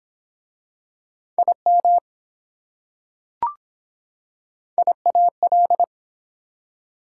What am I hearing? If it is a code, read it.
Morse code, heard as a keyed tone, about 24 wpm, 700 Hz. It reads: IM IAL